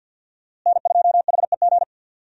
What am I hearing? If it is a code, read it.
Morse code, heard as a keyed tone, about 50 wpm, 700 Hz. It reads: N2HEG
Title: USA Call Signs - 5 Characters - Repeated 3x - 50wpm